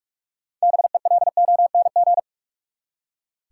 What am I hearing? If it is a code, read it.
Morse code, heard as a keyed tone, about 46 wpm, 700 Hz. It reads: BELONG
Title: Six Letter Words - 45wpm